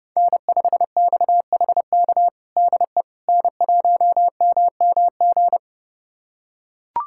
Morse code, heard as a keyed tone, about 32 wpm, 700 Hz. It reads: N5XHK DE N1MMG
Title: USA Callsign DE Callsign - 5 Characters - 30wpm